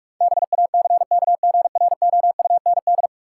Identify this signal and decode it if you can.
Morse code, heard as a keyed tone, about 44 wpm, 700 Hz. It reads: BACKGROUND
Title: Sentences from Top 1300 Words - New Word Drill - 45wpm